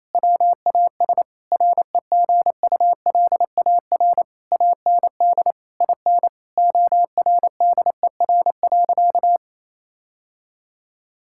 Morse code, heard as a keyed tone, about 28 wpm, 700 Hz. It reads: WAS REGULAR AND IN ORDER.
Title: Sentences from Top 1000 Words - Review Part 6 - 28wpm